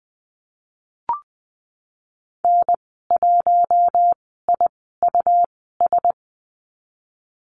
Morse code, heard as a keyed tone, about 20 wpm, spaced as for 15 wpm, 700 Hz. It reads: N1IUS
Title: 173 Lesson 28g - Number 9 - Call Signs 20wpm